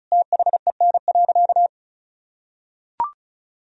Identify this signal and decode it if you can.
Morse code, heard as a keyed tone, about 34 wpm, 700 Hz. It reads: THEN.
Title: Sentences from Top 200 Words and Abbreviations 35wpm